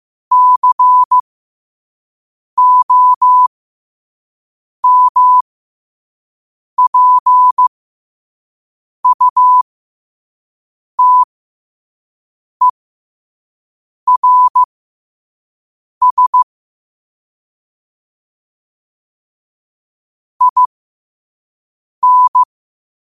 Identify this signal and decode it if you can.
Morse code, heard as a keyed tone, about 15 wpm, spaced as for 5 wpm, 1000 Hz. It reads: COMPUTERS IN
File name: qotdfiles / 5wpm